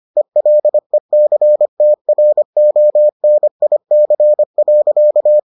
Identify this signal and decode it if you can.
Morse code, heard as a keyed tone, about 25 wpm, 600 Hz. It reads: ELECTRONIC.